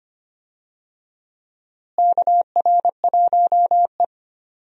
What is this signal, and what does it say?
Morse code, heard as a keyed tone, about 25 wpm, 700 Hz. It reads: KR1E
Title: USA Call Signs - 4 Characters - Repeated 2x - 25wpm